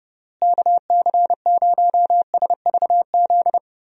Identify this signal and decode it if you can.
Morse code, heard as a keyed tone, about 30 wpm, 700 Hz. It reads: KC0SVZ
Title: USA Call Signs - 6 Characters - Repeated 3x - 30wpm